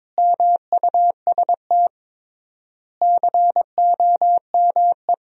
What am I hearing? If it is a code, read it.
Morse code, heard as a keyed tone, about 22 wpm, 700 Hz. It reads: MUST COME
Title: 3x Word-Spacing - Sentences from Top 300 Words - 22wpm